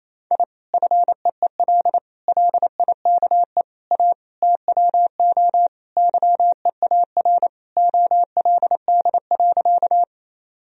Morse code, heard as a keyed tone, about 28 wpm, 700 Hz. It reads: I FEEL LIKE A TWO YEAR OLD.
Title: Sentences from Top 1000 Words - Review Part 1 - 28wpm